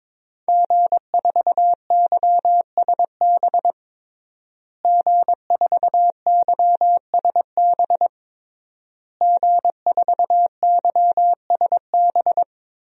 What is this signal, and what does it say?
Morse code, heard as a keyed tone, about 22 wpm, 700 Hz. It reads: G4YSB G4YSB G4YSB
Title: DXCC-100 Call Signs - 5 Characters - Repeated 3x - 22wpm